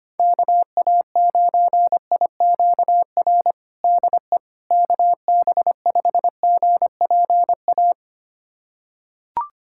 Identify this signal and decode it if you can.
Morse code, heard as a keyed tone, about 25 wpm, 700 Hz. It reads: KA9IQR DE KB5GPA
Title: USA Callsign DE Callsign - 6 Characters - 25wpm